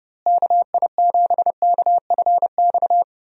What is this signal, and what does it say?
Morse code, heard as a keyed tone, about 32 wpm, 700 Hz. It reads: KI7KFX
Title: USA Call Signs - 6 Characters - Repeated 3x - 30wpm